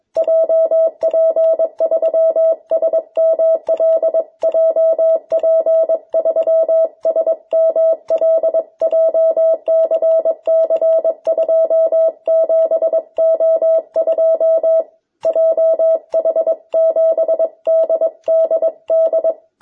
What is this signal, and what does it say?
Morse code, heard as a keyed tone, about 22 wpm, 650 Hz. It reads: JP3SML JP3SML JCC27O2 JH7DDD
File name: jh8cbh / 2021cw